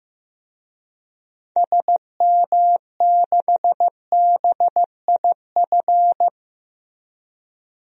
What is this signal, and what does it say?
Morse code, heard as a keyed tone, about 15 wpm, 700 Hz. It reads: SM6BIF
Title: DXCC-100 Call Signs - 6 Characters - 15wpm